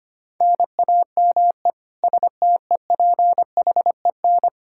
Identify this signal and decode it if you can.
Morse code, heard as a keyed tone, about 25 wpm, 700 Hz. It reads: NAME STEPHEN
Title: Practice QSO - Exchange 1 - Beginning - 25wpm